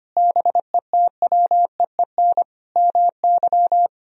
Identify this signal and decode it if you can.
Morse code, heard as a keyed tone, about 25 wpm, 700 Hz. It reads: BETWEEN MY
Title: Sentences from Top 1000 Words - Review Part 3 - 25wpm